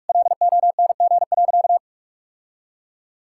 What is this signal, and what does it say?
Morse code, heard as a keyed tone, about 46 wpm, 700 Hz. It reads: LONG.
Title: Sentences from Top 1000 Words - Review Part 5 - 45wpm